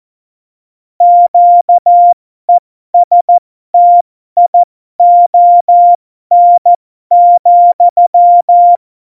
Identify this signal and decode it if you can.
Morse code, heard as a keyed tone, about 14 wpm, spaced as for 12 wpm, 700 Hz. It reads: QESTION,
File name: Morsekurs / Witze